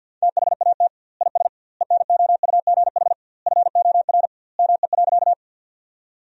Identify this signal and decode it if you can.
Morse code, heard as a keyed tone, about 50 wpm, 700 Hz. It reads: THAT IS ENOUGH FOR ME.
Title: 2x Word-Spacing - Sentences from Top 400 Words - 50wpm